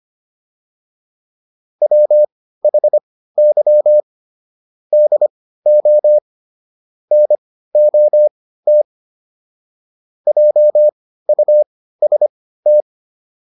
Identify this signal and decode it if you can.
Morse code, heard as a keyed tone, about 25 wpm, spaced as for 15 wpm, 600 Hz. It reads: WHY DO NOT JUST